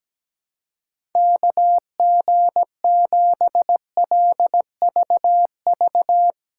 Morse code, heard as a keyed tone, about 18 wpm, 700 Hz. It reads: KG7LVV
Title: Speed Racing - USA 2x3 Call Signs 17wpm